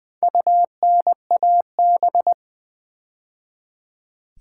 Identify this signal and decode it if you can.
Morse code, heard as a keyed tone, about 20 wpm, 700 Hz. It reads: UNAB
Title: Build A Word - 2 Letters At A Time - 6-Letter Words - 20wpm